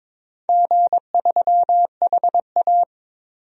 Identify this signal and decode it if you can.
Morse code, heard as a keyed tone, about 22 wpm, 700 Hz. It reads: G3HA
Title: DXCC-100 Call Signs - 4 Characters - Repeated 3x - 22wpm